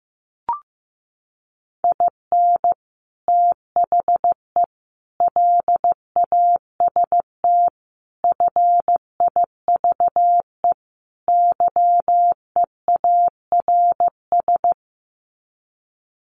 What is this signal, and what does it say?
Morse code, heard as a keyed tone, about 15 wpm, 700 Hz. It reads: IN THE LAST FIVE YEARS
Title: Sets of 5 Words Encore 15wpm